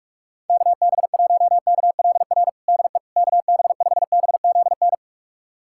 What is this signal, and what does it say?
Morse code, heard as a keyed tone, about 46 wpm, 700 Hz. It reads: KB1KLR DE KB5BZN
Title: USA Callsign DE Callsign - 6 Characters - 45wpm